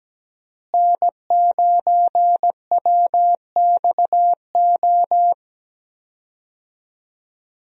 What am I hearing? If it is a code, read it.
Morse code, heard as a keyed tone, about 17 wpm, 700 Hz. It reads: N9WXO
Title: USA Call Signs - 5 Characters - Repeated 3x - 17wpm